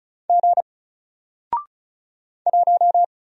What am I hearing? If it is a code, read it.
Morse code, heard as a keyed tone, about 34 wpm, 700 Hz. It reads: G 1
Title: Single Letter-Number Alt 35wpm